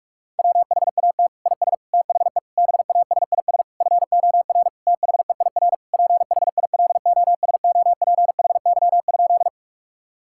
Morse code, heard as a keyed tone, about 46 wpm, 700 Hz. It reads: WHAT IS THE BASIS FOR THEIR PHILOSOPHY?
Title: Sentences from Top 1000 Words - Review Part 6 - 45wpm